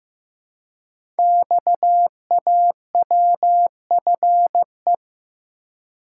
Morse code, heard as a keyed tone, about 15 wpm, 700 Hz. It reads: XAWFE
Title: Code Groups - 5 Letters - 15wpm